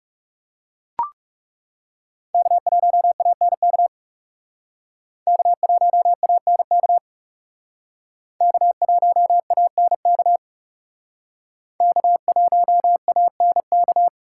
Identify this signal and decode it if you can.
Morse code, heard as a keyed tone, about 38 wpm, 700 Hz. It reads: K1ANK K1ANK K1ANK K1ANK
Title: Speed Racing - USA 1x3 Call Signs 30wpm